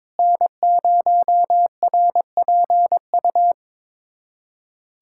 Morse code, heard as a keyed tone, about 22 wpm, 700 Hz. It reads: N0RPU
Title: USA Callsign DE Callsign 22wpm